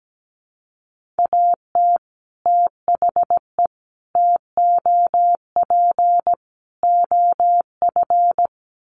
Morse code, heard as a keyed tone, about 17 wpm, 700 Hz. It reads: AT THE TOP OF
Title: Sets of 4 Words 17wpm